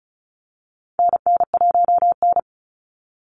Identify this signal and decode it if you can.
Morse code, heard as a keyed tone, about 34 wpm, 700 Hz. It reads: NN1N
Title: Call Signs Repeated 3x - 35wpm